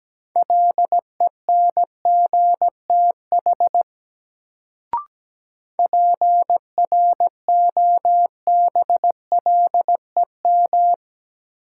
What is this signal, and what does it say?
Morse code, heard as a keyed tone, about 17 wpm, 700 Hz. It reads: LENGTH PROBLEM
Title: Sentences from Top 1000 Words - New Word Drill - 17wpm